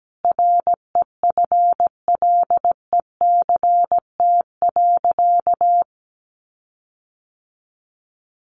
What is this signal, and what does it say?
Morse code, heard as a keyed tone, about 17 wpm, 700 Hz. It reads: REFLECT.
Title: Sentences from Top 1500 Words 17wpm